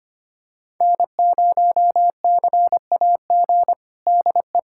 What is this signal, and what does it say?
Morse code, heard as a keyed tone, about 25 wpm, 700 Hz. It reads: N0CAG DE
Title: USA Callsign DE Callsign - 5 Characters - 25wpm